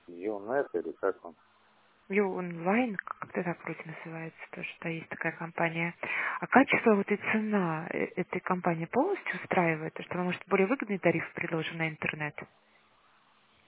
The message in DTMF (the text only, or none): none